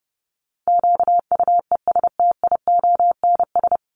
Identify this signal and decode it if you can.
Morse code, heard as a keyed tone, about 30 wpm, 700 Hz. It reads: QUESTIONS
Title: ES:S Suffix - Top 100 Words with Base Words 30wpm